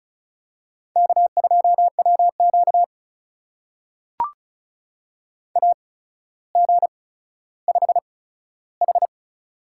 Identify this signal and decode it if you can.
Morse code, heard as a keyed tone, about 36 wpm, spaced as for 15 wpm, 700 Hz. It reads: K2WQ AG5H
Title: USA Call Signs - 4 Characters - 8x Character Spacing - 35wpm